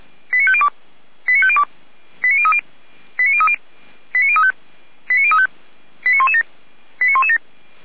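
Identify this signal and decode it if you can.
Five-tone selective call (ZVEI), heard as quick runs of five tones, tones of about 70 ms. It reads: 89502 89502 89030 89030 89035 89035 89117 89117
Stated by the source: ZVEI five-tone selective call 89105 received on an SDR.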